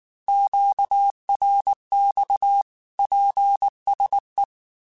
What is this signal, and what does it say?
Morse code, heard as a keyed tone, about 19 wpm, 800 Hz. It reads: QRX PSE
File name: QSOs / QSOs-mp3s / kc7haa@18